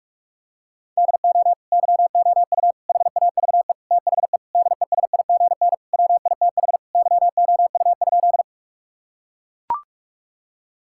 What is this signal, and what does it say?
Morse code, heard as a keyed tone, about 41 wpm, 700 Hz. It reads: DO YOU HAVE THE DESIGN WITH YOU?
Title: Sentences from Top 1000 Words - Review Part 5 - 45wpm